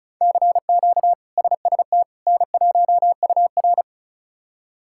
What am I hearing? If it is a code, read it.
Morse code, heard as a keyed tone, about 34 wpm, 700 Hz. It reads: CQ SST N1UR
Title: K1USN SST QSO Practice - 35wpm